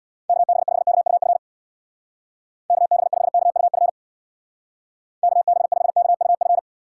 Morse code, heard as a keyed tone, about 64 wpm, 700 Hz. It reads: KB5KUF KB5KUF KB5KUF